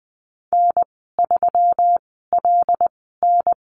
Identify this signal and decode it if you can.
Morse code, heard as a keyed tone, about 19 wpm, spaced as for 14 wpm, 700 Hz. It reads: N3LN
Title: 128 Lesson 21g - Number 6 - Call Signs 20wpm